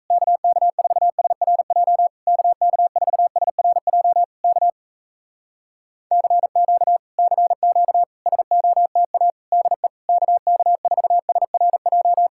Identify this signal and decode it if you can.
Morse code, heard as a keyed tone, about 37 wpm, 700 Hz. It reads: KK4SRJ KK4SRJ K CQ CQ SOTA DE KK4SRJ